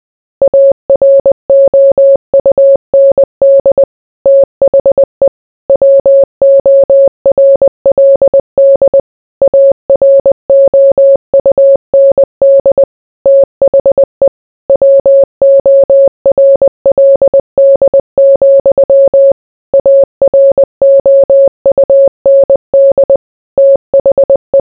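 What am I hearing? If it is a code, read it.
Morse code, heard as a keyed tone, about 20 wpm, 550 Hz. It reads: AROUND THE WORLD AROUND THE WORLD, AROUND THE
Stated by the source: Morse code mixed with noise and music as background sound for sleep.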